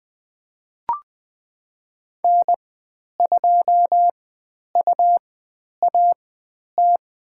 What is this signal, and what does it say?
Morse code, heard as a keyed tone, about 20 wpm, spaced as for 10 wpm, 700 Hz. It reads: N2UAT
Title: 102 Lesson 17g - Letter u - Call Signs 20wpm